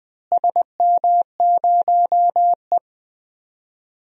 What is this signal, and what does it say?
Morse code, heard as a keyed tone, about 20 wpm, 700 Hz. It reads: SM0E
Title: DXCC-100 Call Signs - 4 Characters - Repeated 3x - 20wpm